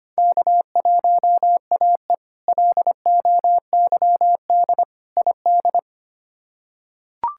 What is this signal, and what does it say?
Morse code, heard as a keyed tone, about 25 wpm, 700 Hz. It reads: K1AE LOYD ID
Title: K1USN SST QSO Practice - 25wpm